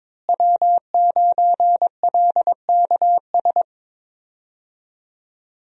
Morse code, heard as a keyed tone, about 21 wpm, 700 Hz. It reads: W9LKS
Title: USA Call Signs - 5 Characters - Repeated 2x - 22wpm